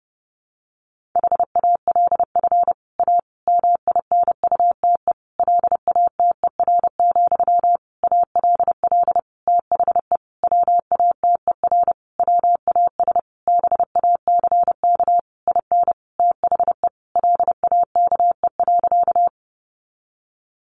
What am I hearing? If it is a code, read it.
Morse code, heard as a keyed tone, about 30 wpm, 700 Hz. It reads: HALF A MINUTE LATER, ALL THE WATER WAS BACK IN THE LAKE.
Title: Sentences from Top 1100 Words 30wpm